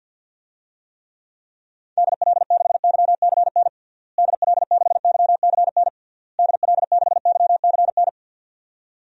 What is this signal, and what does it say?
Morse code, heard as a keyed tone, about 48 wpm, 700 Hz. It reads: DL6YCN DL6YCN DL6YCN
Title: DXCC-100 Call Signs - 6 Characters - Repeated 3x - 50wpm